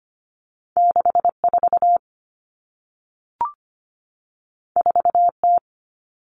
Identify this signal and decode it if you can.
Morse code, heard as a keyed tone, about 25 wpm, 700 Hz. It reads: 64 4T
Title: All Cut Numbers - 2-Digit 25wpm